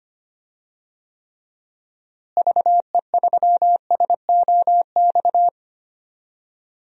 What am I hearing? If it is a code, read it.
Morse code, heard as a keyed tone, about 25 wpm, 700 Hz. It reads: VE3SOX
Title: DXCC-100 Call Signs - 6 Characters - Repeated 3x - 25wpm